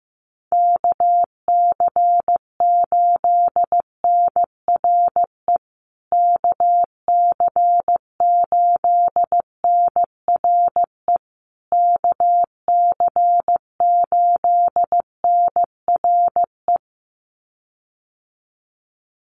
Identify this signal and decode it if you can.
Morse code, heard as a keyed tone, about 15 wpm, 700 Hz. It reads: KC8NRE KC8NRE KC8NRE
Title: USA Call Signs Repeated 3x 15wpm